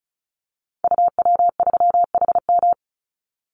Morse code, heard as a keyed tone, about 36 wpm, 700 Hz. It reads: UW3HM
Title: Call Signs - 35wpm